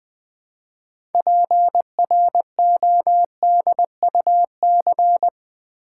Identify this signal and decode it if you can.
Morse code, heard as a keyed tone, about 20 wpm, 700 Hz. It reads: PRODUC
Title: Build A Word - 2 Letters At A Time - 8-Letter Words - 20wpm